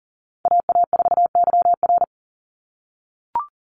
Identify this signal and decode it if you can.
Morse code, heard as a keyed tone, about 40 wpm, 700 Hz. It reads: AA4YR
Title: USA Call Signs 40wpm